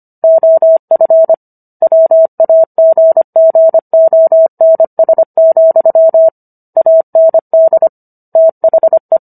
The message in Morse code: OF WAGGONS, AND THE

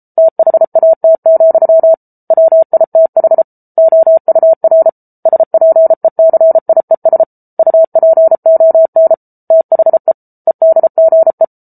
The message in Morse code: THAT, WITH OUR SPECIES UPON THE EDGE

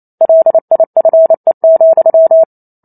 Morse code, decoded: LIFE,

